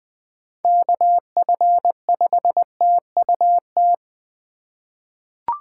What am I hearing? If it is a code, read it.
Morse code, heard as a keyed tone, about 20 wpm, 700 Hz. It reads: KF5TUT
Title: USA Call Signs - 6 Characters - 3x Character Spacing - 20wpm